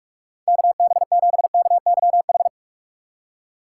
Morse code, heard as a keyed tone, about 46 wpm, 700 Hz. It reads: KB7KYH
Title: USA Call Signs - 3x Character Spacing - 45wpm